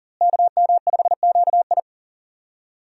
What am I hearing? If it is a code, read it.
Morse code, heard as a keyed tone, about 40 wpm, 700 Hz. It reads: KM5QI
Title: USA Call Signs - 5 Characters - Repeated 3x - 40wpm